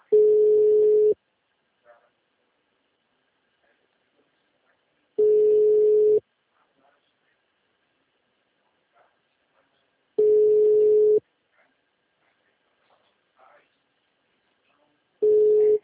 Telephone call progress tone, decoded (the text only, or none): ringback tone